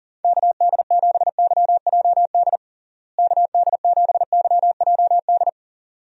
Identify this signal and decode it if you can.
Morse code, heard as a keyed tone, about 40 wpm, 700 Hz. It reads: KD7YJD KD7YJD
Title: USA Call Signs - 6 Characters - Repeated 2x - 40wpm